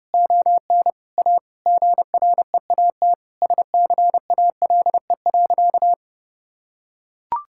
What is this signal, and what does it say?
Morse code, heard as a keyed tone, about 30 wpm, 700 Hz. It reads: ON A GREAT SCALE.
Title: Sentences from Top 1000 Words - Review Part 5 - 30wpm